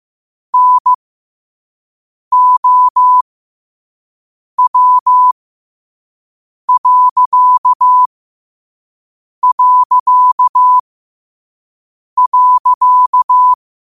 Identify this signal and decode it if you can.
Morse code, heard as a keyed tone, about 15 wpm, spaced as for 5 wpm, 1000 Hz. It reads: NOW...
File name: qotdfiles / 5wpm